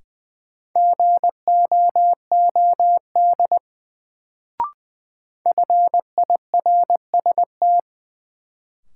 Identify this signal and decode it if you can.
Morse code, heard as a keyed tone, about 20 wpm, 700 Hz. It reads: GOOD FIRST
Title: Sets of 2 Words - New Word Drill - 20wpm